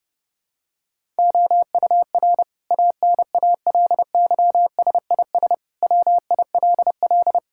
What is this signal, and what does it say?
Morse code, heard as a keyed tone, about 30 wpm, 700 Hz. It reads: OUR ANALYSIS WILL
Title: Sentences from Top 1000 Words - Review Part 5 - 30wpm